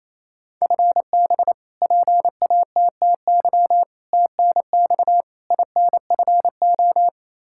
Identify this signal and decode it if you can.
Morse code, heard as a keyed tone, about 28 wpm, 700 Hz. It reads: FB PATTY TNX INFO
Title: K1USN SST QSO Practice 28wpm